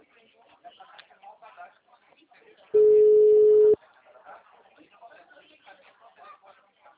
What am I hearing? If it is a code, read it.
Ringback tone (Europe, 425 Hz), with speech.